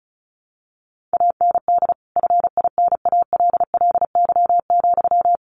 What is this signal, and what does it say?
Morse code, heard as a keyed tone, about 34 wpm, 700 Hz. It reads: AND FINALLY,
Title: Sentences from Top 1100 Words 35wpm